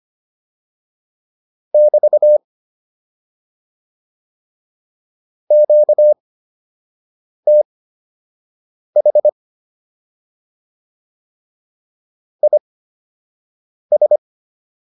Morse code, heard as a keyed tone, about 25 wpm, spaced as for 6 wpm, 600 Hz. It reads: <BT> QTH IS